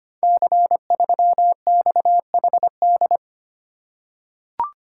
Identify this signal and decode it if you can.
Morse code, heard as a keyed tone, about 25 wpm, 700 Hz. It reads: C3XHD
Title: Code Groups - 5 Letter:Numbers - 25wpm